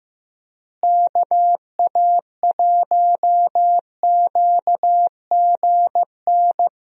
Morse code, heard as a keyed tone, about 15 wpm, 700 Hz. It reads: KA1QGN
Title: USA Call Signs - 6 Characters - Repeated 3x - 15wpm